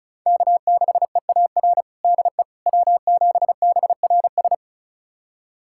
Morse code, heard as a keyed tone, about 36 wpm, 700 Hz. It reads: K6EAR DE W7BRS
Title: USA Callsign DE Callsign - 5 Characters - 35wpm